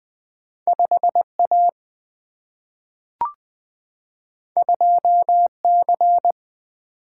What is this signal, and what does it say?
Morse code, heard as a keyed tone, about 20 wpm, 700 Hz. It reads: 5 A 2 C
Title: Single-Letter Single-Number 20wpm